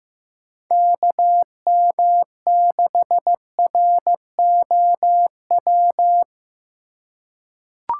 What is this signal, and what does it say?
Morse code, heard as a keyed tone, about 15 wpm, 700 Hz. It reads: KM6ROW
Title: USA Call Signs - 6 Characters - 2x Character Spacing - 15wpm